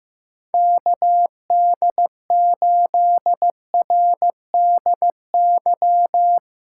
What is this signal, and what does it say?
Morse code, heard as a keyed tone, about 15 wpm, 700 Hz. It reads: KD8RDY
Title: USA Call Signs - 4x Character Spacing - 15wpm